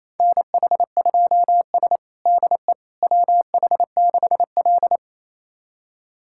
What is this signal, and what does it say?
Morse code, heard as a keyed tone, about 28 wpm, 700 Hz. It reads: NH2S DE WH6L
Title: USA Callsign DE Callsign - 4 Characters - 28wpm